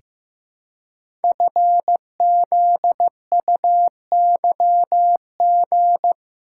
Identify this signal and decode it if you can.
Morse code, heard as a keyed tone, about 15 wpm, 700 Hz. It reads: FZUYG
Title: Code Groups - 5 Letter:Numbers - 15wpm